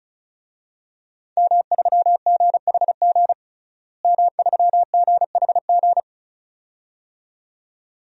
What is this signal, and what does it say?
Morse code, heard as a keyed tone, about 34 wpm, 700 Hz. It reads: M3GHG M3GHG
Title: DXCC-100 Call Signs - 5 Characters - Repeated 2x - 35wpm